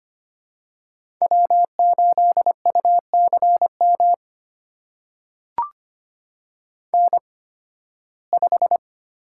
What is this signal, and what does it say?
Morse code, heard as a keyed tone, about 25 wpm, spaced as for 16 wpm, 700 Hz. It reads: W8UCM N5
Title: USA Call Signs - 5 Characters - 8x Character Spacing - 25wpm